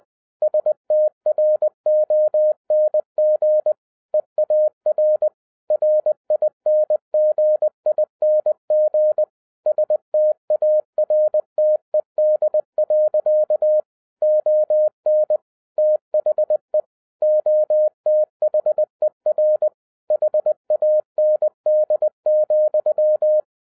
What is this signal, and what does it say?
Morse code, heard as a keyed tone, about 20 wpm, 610 Hz. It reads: STRONG EAR RINGING STARTED. ON THE OTHER HAND,
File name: 2011-07-mp3 / n58 BACK FROM A BIG ACCIDENT 20wpm